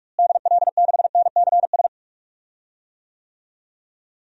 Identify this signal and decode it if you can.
Morse code, heard as a keyed tone, about 41 wpm, 700 Hz. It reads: DL6NCS
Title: DXCC-100 Call Signs - 6 Characters - Repeated 2x - 45wpm